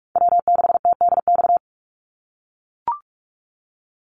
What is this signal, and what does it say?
Morse code, heard as a keyed tone, about 44 wpm, 690 Hz. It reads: W6TDX
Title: NA SOTA Chaser Callsigns 45wpm